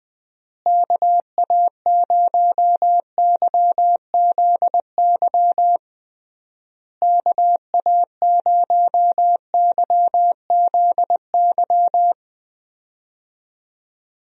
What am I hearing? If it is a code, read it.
Morse code, heard as a keyed tone, about 20 wpm, 700 Hz. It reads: KA0YZY KA0YZY
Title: USA Call Signs - 6 Characters - Repeated 3x - 20wpm